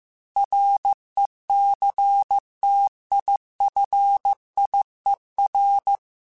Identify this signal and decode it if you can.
Morse code, heard as a keyed tone, about 15 wpm, 800 Hz. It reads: RECTIFIER
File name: Technical Subjects / Technical Subjects-mp3s / Sw Pwr Sup @13